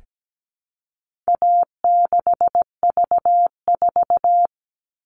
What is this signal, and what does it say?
Morse code, heard as a keyed tone, about 17 wpm, 700 Hz. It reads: A6V4
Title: All Cut Numbers - 4-Digit 17wpm